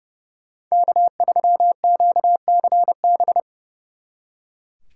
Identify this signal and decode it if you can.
Morse code, heard as a keyed tone, about 36 wpm, 700 Hz. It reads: K3QCB
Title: Speed Racing - USA Call Signs 30wpm